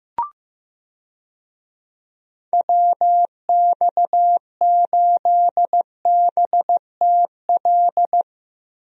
Morse code, heard as a keyed tone, about 15 wpm, 700 Hz. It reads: WX8BTL
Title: USA Call Signs - 6 Characters - Repeated 3x - 15wpm